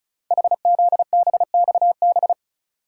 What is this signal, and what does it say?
Morse code, heard as a keyed tone, about 36 wpm, 700 Hz. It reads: HZBXB
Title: Code Groups - 5 Letters - 35wpm